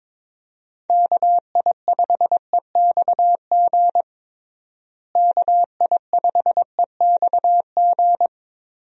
Morse code, heard as a keyed tone, about 22 wpm, 700 Hz. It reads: KI5EXG KI5EXG
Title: USA Call Signs - 6 Characters - Repeated 2x - 22wpm